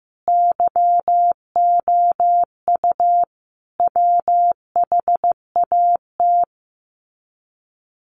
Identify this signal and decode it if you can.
Morse code, heard as a keyed tone, about 15 wpm, 700 Hz. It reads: YOU WHAT
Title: Sets of 5 Words Encore 15wpm